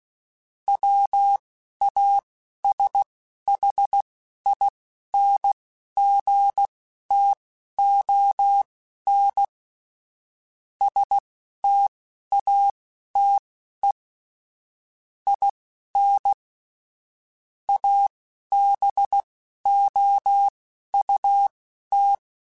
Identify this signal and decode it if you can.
Morse code, heard as a keyed tone, about 16 wpm, spaced as for 11 wpm, 800 Hz. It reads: WASHINGTON STATE IN ABOUT
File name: QSOs / QSOs-mp3s / WD0BGP 1@13